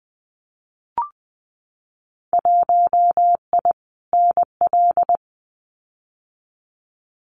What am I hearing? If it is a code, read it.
Morse code, heard as a keyed tone, about 20 wpm, 700 Hz. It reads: 1I NL
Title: Winter Field Day Exchanges Encore 20wpm